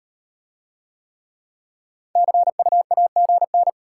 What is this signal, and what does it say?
Morse code, heard as a keyed tone, about 38 wpm, 700 Hz. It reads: CUAGN